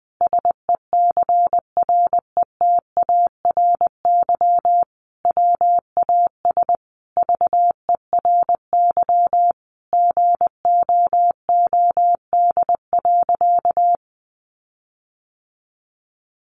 Morse code, heard as a keyed tone, about 20 wpm, 700 Hz. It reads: SECRETARY WAS VERY GOOD.
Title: Sentences from Top 1500 Words 20wpm